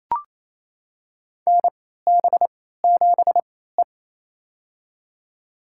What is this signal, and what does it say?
Morse code, heard as a keyed tone, about 28 wpm, spaced as for 16 wpm, 700 Hz. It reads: NB7E
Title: USA Call Signs - 4 Characters - 3x Character Spacing - 28wpm